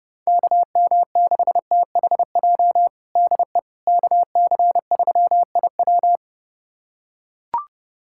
Morse code, heard as a keyed tone, about 30 wpm, 700 Hz. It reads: KM6THJ DE KC3IW
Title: USA Callsign DE Callsign 30wpm